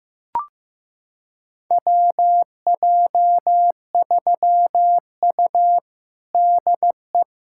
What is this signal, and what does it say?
Morse code, heard as a keyed tone, about 16 wpm, 700 Hz. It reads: WJ3U DE
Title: USA Callsign DE Callsign - 4 Characters - 15wpm